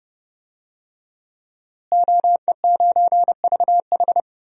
Morse code, heard as a keyed tone, about 30 wpm, 700 Hz. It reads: OE9VH
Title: DXCC-100 Call Signs - 5 Characters - Repeated 3x - 30wpm